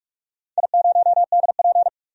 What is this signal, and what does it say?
Morse code, heard as a keyed tone, about 44 wpm, 700 Hz. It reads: I0DP
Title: DXCC-100 Call Signs - 4 Characters - 45wpm